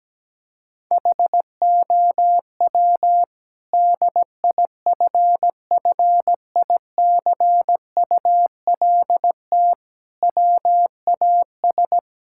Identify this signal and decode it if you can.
Morse code, heard as a keyed tone, about 17 wpm, 700 Hz. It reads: HOW DIFFICULT WAS
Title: Sentences from Top 1000 Words - Review Part 6 - 17wpm